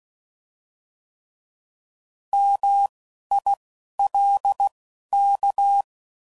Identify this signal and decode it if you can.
Morse code, heard as a keyed tone, about 16 wpm, spaced as for 11 wpm, 800 Hz. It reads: MILK